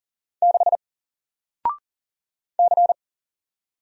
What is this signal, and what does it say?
Morse code, heard as a keyed tone, about 40 wpm, 700 Hz. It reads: 6 C
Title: Single Letter-Number 40wpm